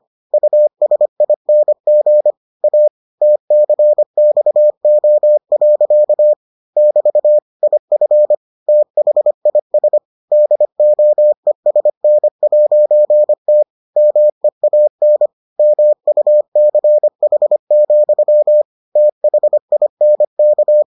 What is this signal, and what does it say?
Morse code, heard as a keyed tone, about 25 wpm, 600 Hz. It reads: USING A TCXO. <BT> IF THIS DOESN'T MEAN MUCH, THINK